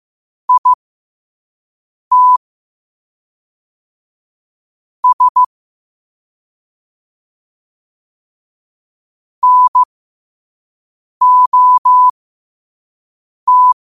Morse code, heard as a keyed tone, about 15 wpm, spaced as for 5 wpm, 1000 Hz. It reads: IT S NOT
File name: qotdfiles / 5wpm